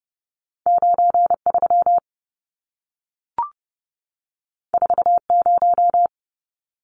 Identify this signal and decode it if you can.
Morse code, heard as a keyed tone, about 30 wpm, 700 Hz. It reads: 93 40
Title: Numbers - 2-Digit 30wpm